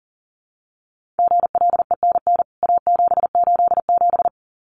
Morse code, heard as a keyed tone, about 40 wpm, 700 Hz. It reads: GLENN A787
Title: CWT Exchange 40wpm